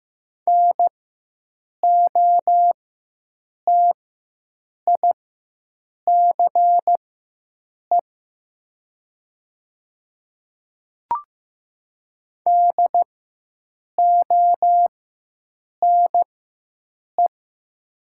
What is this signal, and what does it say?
Morse code, heard as a keyed tone, about 15 wpm, spaced as for 7 wpm, 700 Hz. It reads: NOTICE DONE
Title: 4x Character Spacing - Top 500 Words - 15wpm